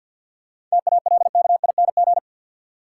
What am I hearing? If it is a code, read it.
Morse code, heard as a keyed tone, about 47 wpm, 700 Hz. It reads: TALKING
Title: Sets of 2 Words - New Word Drill - 50wpm